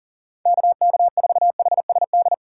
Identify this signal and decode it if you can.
Morse code, heard as a keyed tone, about 40 wpm, 700 Hz. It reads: KK4HSD
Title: USA Call Signs - 6 Characters - 2x Character Spacing - 40wpm